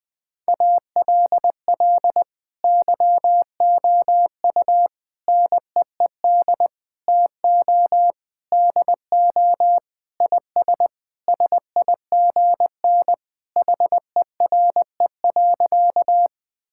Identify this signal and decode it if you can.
Morse code, heard as a keyed tone, about 20 wpm, 700 Hz. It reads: ALL YOU NEED TO DO IS SIGN HERE.
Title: Sentences with 6-Word Phrases - Spoken Beginning - 20wpm